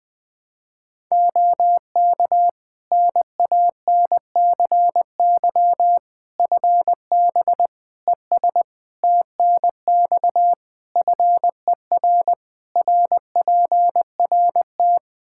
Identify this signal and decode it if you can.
Morse code, heard as a keyed tone, about 21 wpm, 700 Hz. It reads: OK NANCY FB ES TNX FER RPRT
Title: Practice QSO - Exchange 2 - Info - 20wpm